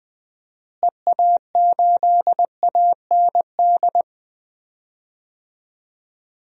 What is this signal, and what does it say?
Morse code, heard as a keyed tone, about 21 wpm, 700 Hz. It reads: EA8AND
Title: DXCC-100 Call Signs - 6 Characters - Repeated 2x - 20wpm